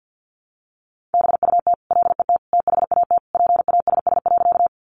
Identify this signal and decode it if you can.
Morse code, heard as a keyed tone, about 50 wpm, 700 Hz. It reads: BUT LET THAT PASS.
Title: Sentences from Top 400 Words 50wpm